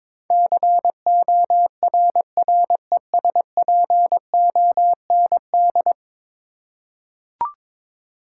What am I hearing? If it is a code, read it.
Morse code, heard as a keyed tone, about 22 wpm, 700 Hz. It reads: CORRESPOND
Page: morse-code-ninja-practice-22wpm